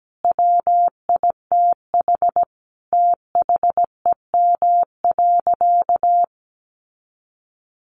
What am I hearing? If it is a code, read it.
Morse code, heard as a keyed tone, about 17 wpm, 700 Hz. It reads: WITH THEM.
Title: Sentences from Top 700 Words 17wpm